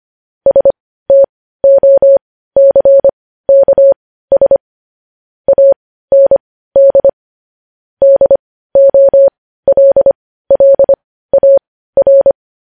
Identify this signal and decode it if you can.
Morse code, heard as a keyed tone, about 25 wpm, spaced as for 15 wpm, 550 Hz. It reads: STOCKS AND DOLLAR
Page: news headlines in morse code at 15 wpm